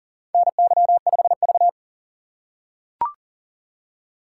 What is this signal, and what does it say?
Morse code, heard as a keyed tone, about 40 wpm, 700 Hz. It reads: NY5V
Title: USA Call Signs - 4 Characters - 6x Character Spacing - 40wpm